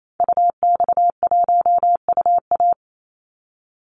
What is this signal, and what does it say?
Morse code, heard as a keyed tone, about 28 wpm, 700 Hz. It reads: UX1UA
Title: Call Signs - 28wpm